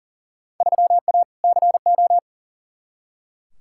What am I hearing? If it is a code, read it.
Morse code, heard as a keyed tone, about 40 wpm, 700 Hz. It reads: 3A CO
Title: Field Day Exchanges Encore - 40wpm